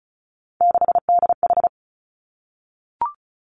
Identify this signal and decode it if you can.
Morse code, heard as a keyed tone, about 36 wpm, 700 Hz. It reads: 6DH
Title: Three Challenging Letter-Number Combinations 35wpm